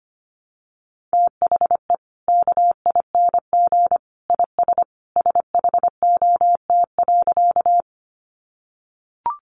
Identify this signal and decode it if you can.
Morse code, heard as a keyed tone, about 25 wpm, 700 Hz. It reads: THE KING IS SHOT.
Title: Sentences from Top 1100 Words 25wpm